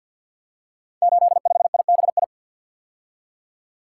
Morse code, heard as a keyed tone, about 52 wpm, 700 Hz. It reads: 85IBI
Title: Code Groups - 5 Letter:Numbers - 50wpm